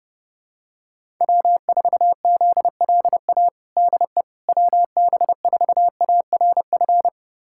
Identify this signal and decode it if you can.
Morse code, heard as a keyed tone, about 30 wpm, 700 Hz. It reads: W4ZLA DE WB4ARF
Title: USA Callsign DE Callsign 30wpm